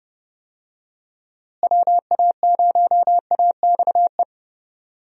Morse code, heard as a keyed tone, about 30 wpm, 700 Hz. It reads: WA0AXE
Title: USA Call Signs - 6 Characters - Repeated 2x - 30wpm